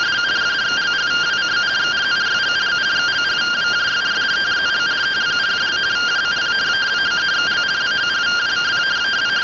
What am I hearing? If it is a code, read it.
Two-tone teleprinter signal, about 170 Hz shift.